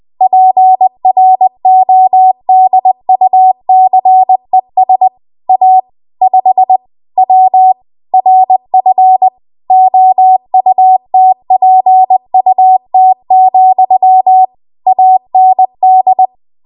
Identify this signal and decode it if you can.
Morse code, heard as a keyed tone, about 20 wpm, 750 Hz. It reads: PRODUCES A 5 W RF OUTPUT, AND
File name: Archive / 20 WPM / 20WPM